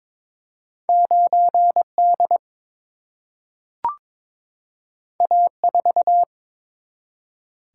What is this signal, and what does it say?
Morse code, heard as a keyed tone, about 22 wpm, 700 Hz. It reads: 9 D A 4
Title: Single-Letter Single-Number 22wpm